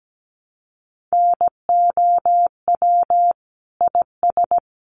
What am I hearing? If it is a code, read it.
Morse code, heard as a keyed tone, about 17 wpm, 700 Hz. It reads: NOW IS
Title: Sentences from Top 100 Words 17wpm